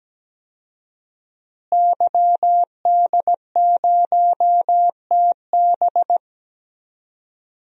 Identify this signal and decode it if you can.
Morse code, heard as a keyed tone, about 17 wpm, 700 Hz. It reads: YD0TB
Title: DXCC-100 Call Signs - 5 Characters - Repeated 2x - 17wpm